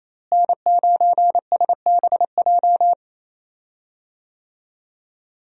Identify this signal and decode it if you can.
Morse code, heard as a keyed tone, about 27 wpm, 700 Hz. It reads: N9SBJ
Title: USA Call Signs - 5 Characters - Repeated 3x - 28wpm